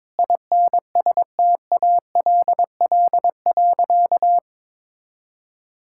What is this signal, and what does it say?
Morse code, heard as a keyed tone, about 22 wpm, 700 Hz. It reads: INSTALL.
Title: Sentences from Top 1600 Words 22wpm